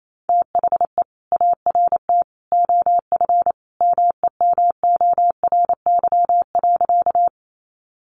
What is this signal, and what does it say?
Morse code, heard as a keyed tone, about 28 wpm, 700 Hz. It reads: THE ART OF MEMORY.
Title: Sentences from Top 1300 Words 28wpm